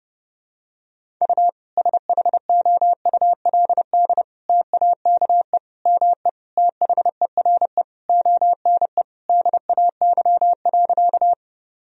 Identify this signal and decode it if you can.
Morse code, heard as a keyed tone, about 30 wpm, 700 Hz. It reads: U SHOULD TAKE ME THERE ONE DAY.
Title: Sentences from Top 300 Words and Abbreviations 30wpm